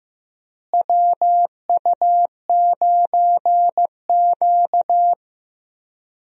Morse code, heard as a keyed tone, about 15 wpm, 700 Hz. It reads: WU9Q
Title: USA Call Signs - 4 Characters - Repeated 2x - 15wpm